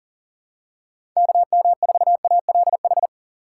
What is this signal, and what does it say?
Morse code, heard as a keyed tone, about 40 wpm, 700 Hz. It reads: KM4ALH
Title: USA Call Signs - 6 Characters - 2x Character Spacing - 40wpm